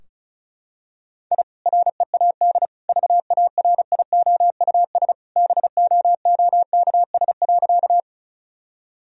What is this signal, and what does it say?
Morse code, heard as a keyed tone, about 34 wpm, 700 Hz. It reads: I READ VARIOUS BOOKS.
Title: Sentences from Top 1000 Words - Review Part 4 - 35wpm